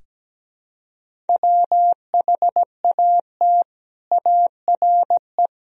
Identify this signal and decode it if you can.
Morse code, heard as a keyed tone, about 17 wpm, 700 Hz. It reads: WHAT ARE
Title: Sentences from Top 1000 Words - Review Part 1 - 17wpm